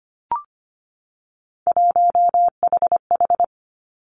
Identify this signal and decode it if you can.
Morse code, heard as a keyed tone, about 25 wpm, 700 Hz. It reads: 1HH
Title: Three Challenging Letter-Number Combinations 25wpm